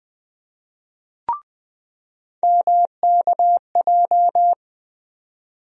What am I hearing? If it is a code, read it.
Morse code, heard as a keyed tone, about 20 wpm, 700 Hz. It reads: MKJ
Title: USA Call Sign Postfixes - 3 Letters 20wpm